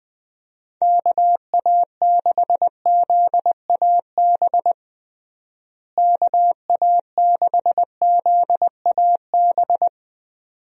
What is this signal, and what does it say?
Morse code, heard as a keyed tone, about 21 wpm, 700 Hz. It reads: KA6ZAB KA6ZAB
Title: USA Call Signs - 6 Characters - Repeated 2x - 20wpm